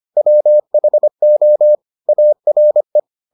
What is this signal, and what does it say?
Morse code, heard as a keyed tone, about 25 wpm, 600 Hz. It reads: WHO ARE